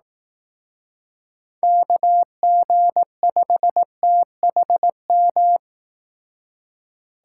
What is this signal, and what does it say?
Morse code, heard as a keyed tone, about 18 wpm, 700 Hz. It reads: KG5THM
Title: Speed Racing - USA 2x3 Call Signs 15wpm